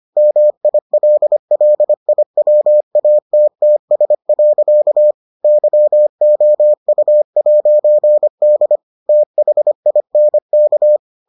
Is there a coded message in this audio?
Yes: Morse code from 0.2 to 11.0 s, heard as a keyed tone, about 25 wpm, 600 Hz. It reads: MILLIWATTS. YOU'D THINK